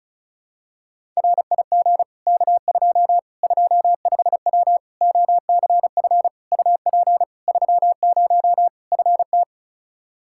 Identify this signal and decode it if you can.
Morse code, heard as a keyed tone, about 36 wpm, 700 Hz. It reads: RIG K2 25W OCF UP 30 FT
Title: Practice QSO - Exchange 2 - Info - 35wpm